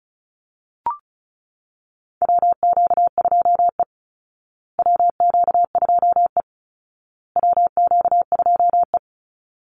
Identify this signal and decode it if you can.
Morse code, heard as a keyed tone, about 34 wpm, 700 Hz. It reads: WQ2E WQ2E WQ2E
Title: Call Signs Repeated 2x - 35wpm